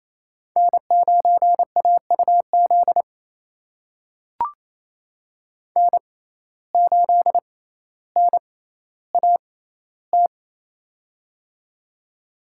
Morse code, heard as a keyed tone, about 28 wpm, spaced as for 14 wpm, 700 Hz. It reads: N9AUZ N8NAT
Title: USA Call Signs - 5 Characters - 6x Character Spacing - 28wpm